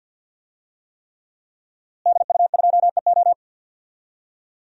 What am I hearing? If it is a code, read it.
Morse code, heard as a keyed tone, about 53 wpm, 700 Hz. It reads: DU2EO
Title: DXCC-100 Call Signs - 5 Characters - Repeated 3x - 50wpm